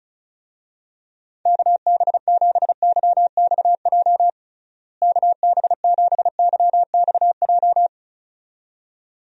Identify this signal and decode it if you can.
Morse code, heard as a keyed tone, about 34 wpm, 700 Hz. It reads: KB7YXJ KB7YXJ
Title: USA Call Signs - 6 Characters - Repeated 2x - 35wpm